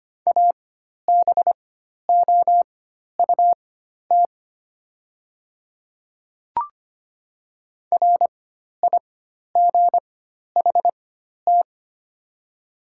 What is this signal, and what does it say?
Morse code, heard as a keyed tone, about 25 wpm, spaced as for 12 wpm, 700 Hz. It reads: ABOUT RIGHT
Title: 4x Character Spacing - Top 300 Words - 25wpm